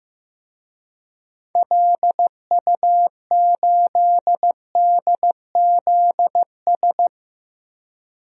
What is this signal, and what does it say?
Morse code, heard as a keyed tone, about 16 wpm, 700 Hz. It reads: LU8DZS
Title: DXCC-100 Call Signs - 6 Characters - 15wpm